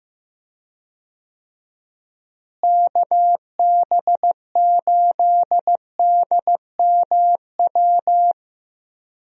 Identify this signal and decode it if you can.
Morse code, heard as a keyed tone, about 15 wpm, 700 Hz. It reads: KB8DMW
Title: USA Call Signs - 6 Characters - Repeated 2x - 15wpm